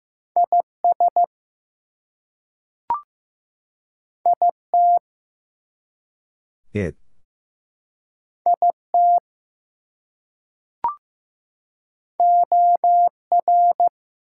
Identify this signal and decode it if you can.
Morse code, heard as a keyed tone, about 15 wpm, 700 Hz. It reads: IS IT IT OR
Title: Two Letter Words - 15wpm